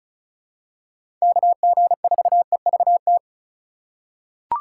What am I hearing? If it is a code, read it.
Morse code, heard as a keyed tone, about 34 wpm, 700 Hz. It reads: KG4EVT
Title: USA Call Signs - 6 Characters - 2x Character Spacing - 35wpm